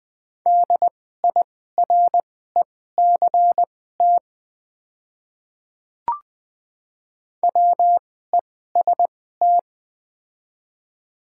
Morse code, heard as a keyed tone, about 20 wpm, spaced as for 14 wpm, 700 Hz. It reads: DIRECT WEST
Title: 2x Character Spacing - Top 500 Words - 20wpm